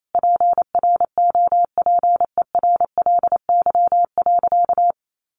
Morse code, heard as a keyed tone, about 28 wpm, 700 Hz. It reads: PROPERLY.
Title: Sentences from Top 1100 Words 28wpm